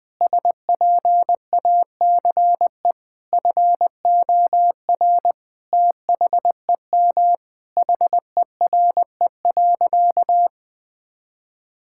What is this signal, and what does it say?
Morse code, heard as a keyed tone, about 20 wpm, 700 Hz. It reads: SPACE FOR THEM HERE.
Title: Sentences from Top 1000 Words - Review Part 5 - 20wpm